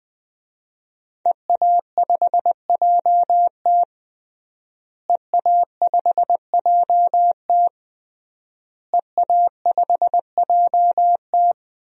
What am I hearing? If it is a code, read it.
Morse code, heard as a keyed tone, about 20 wpm, 700 Hz. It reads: EA5JT EA5JT EA5JT
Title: DXCC-100 Call Signs - 5 Characters - Repeated 3x - 20wpm